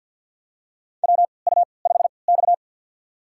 Morse code, heard as a keyed tone, about 48 wpm, spaced as for 28 wpm, 700 Hz. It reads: WU5X
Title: USA Call Signs - 4 Characters - 3x Character Spacing - 50wpm